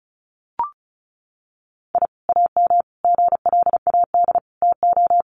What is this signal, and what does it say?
Morse code, heard as a keyed tone, about 34 wpm, 700 Hz. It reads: I AM GLAD TO
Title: Sentences from Top 1300 Words 35wpm